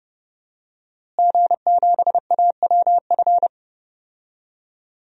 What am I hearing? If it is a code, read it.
Morse code, heard as a keyed tone, about 32 wpm, 700 Hz. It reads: G7AWF
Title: DXCC-100 Call Signs - 5 Characters - Repeated 2x - 30wpm